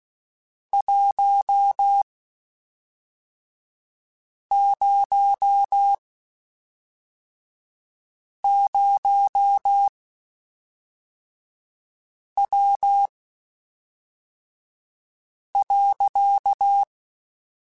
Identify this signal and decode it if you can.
Morse code, heard as a keyed tone, about 16 wpm, spaced as for 3 wpm, 800 Hz. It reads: 100W.
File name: (spaced) / QSOs Spaced-mp3s / WD0FBT 1 @5 sp